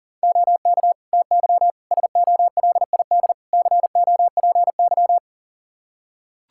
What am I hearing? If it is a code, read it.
Morse code, heard as a keyed tone, about 40 wpm, 700 Hz. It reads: OK TY SOLID COPY
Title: K1USN SST QSO Practice 40wpm